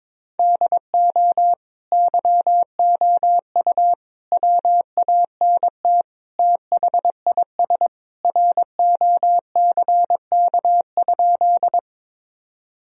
Morse code, heard as a keyed tone, about 22 wpm, 700 Hz. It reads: DO YOU WANT THIS ROCK?
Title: Sentences from Top 1000 Words - Review Part 2 - 22wpm